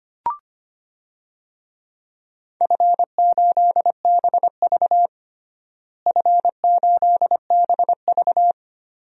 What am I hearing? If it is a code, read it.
Morse code, heard as a keyed tone, about 25 wpm, 700 Hz. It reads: F8BV F8BV
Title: DXCC-100 Call Signs - 4 Characters - Repeated 3x - 25wpm